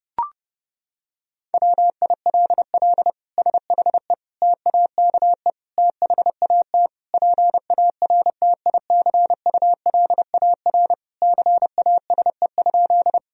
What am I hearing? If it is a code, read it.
Morse code, heard as a keyed tone, about 32 wpm, 700 Hz. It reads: WILL SHE TAKE THAT PARTICULAR CASE?
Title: Sentences from Top 1000 Words - Review Part 4 - 30wpm